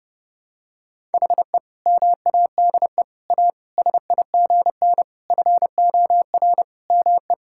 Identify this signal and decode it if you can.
Morse code, heard as a keyed tone, about 30 wpm, 700 Hz. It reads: HE MADE A SIGN FOR ME
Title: Sentences from Top 1000 Words - Review Part 6 - 30wpm